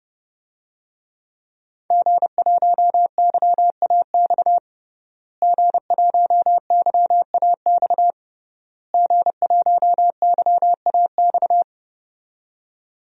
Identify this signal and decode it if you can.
Morse code, heard as a keyed tone, about 30 wpm, 700 Hz. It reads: G1YAX G1YAX G1YAX
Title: DXCC-100 Call Signs - 5 Characters - Repeated 3x - 30wpm